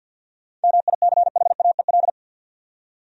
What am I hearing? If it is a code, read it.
Morse code, heard as a keyed tone, about 47 wpm, 700 Hz. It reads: MICHAEL
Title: USA Common Names - 50wpm